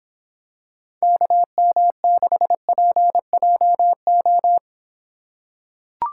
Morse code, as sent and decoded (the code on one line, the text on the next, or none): -.- -- -.... .--. .--- ---
KM6PJO